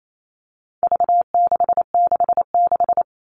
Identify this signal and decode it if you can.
Morse code, heard as a keyed tone, about 28 wpm, 700 Hz. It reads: V666
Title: All Cut Numbers - 4-Digit 28wpm